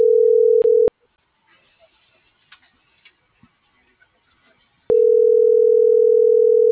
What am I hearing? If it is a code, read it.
Ringback tone (North America), 440 + 480 Hz.